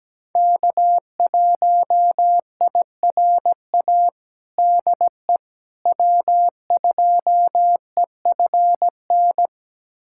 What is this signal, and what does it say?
Morse code, heard as a keyed tone, about 17 wpm, 700 Hz. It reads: K1IRA DE W2EFN
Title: USA Callsign DE Callsign - 5 Characters - 17wpm